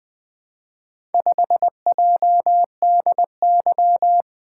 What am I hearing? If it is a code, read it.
Morse code, heard as a keyed tone, about 20 wpm, 700 Hz. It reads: 5JDY
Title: Four Letter-Number Combinations - 20wpm